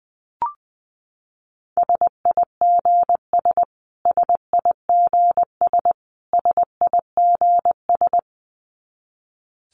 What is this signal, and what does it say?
Morse code, heard as a keyed tone, about 20 wpm, 700 Hz. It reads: SIGS SIGS SIGS
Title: Abbreviations Repeated 3x - 20wpm